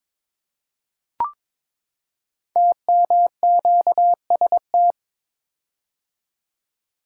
Morse code, heard as a keyed tone, about 22 wpm, 700 Hz. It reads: TMQST
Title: Code Groups - 5 Letter:Numbers - 22wpm